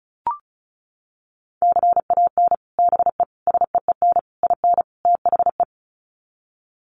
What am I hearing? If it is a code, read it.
Morse code, heard as a keyed tone, about 34 wpm, 700 Hz. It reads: CAN BE SEEN IN THE
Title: Sets of 5 Words Encore2 35wpm